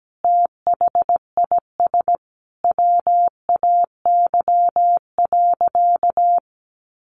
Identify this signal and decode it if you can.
Morse code, heard as a keyed tone, about 17 wpm, 700 Hz. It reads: THIS WAY.
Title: Sentences from Top 100 Words 17wpm